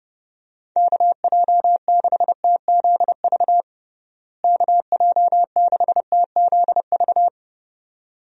Morse code, heard as a keyed tone, about 30 wpm, 700 Hz. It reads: KJ6TZV KJ6TZV
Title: USA Call Signs - 6 Characters - Repeated 2x - 30wpm